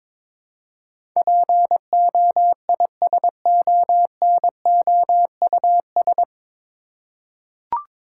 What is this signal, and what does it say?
Morse code, heard as a keyed tone, about 22 wpm, 700 Hz. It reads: POISONOUS
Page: Morse Code Ninja practice 22wpm